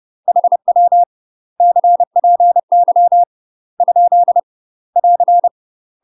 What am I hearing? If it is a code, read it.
Morse code, heard as a keyed tone, about 30 wpm, 700 Hz. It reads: HW CPY ? <AR>
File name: UK MASTER QSOs TRAINING / 30WPM / UK MASTER QSOs 0039